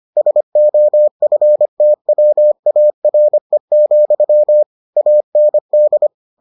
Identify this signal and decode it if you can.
Morse code, heard as a keyed tone, about 25 wpm, 600 Hz. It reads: SOFTWARE, AND